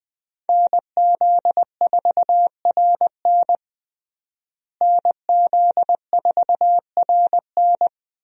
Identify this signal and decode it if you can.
Morse code, heard as a keyed tone, about 20 wpm, 700 Hz. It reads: NZ4RN NZ4RN
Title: USA Call Signs - 5 Characters - Repeated 2x - 20wpm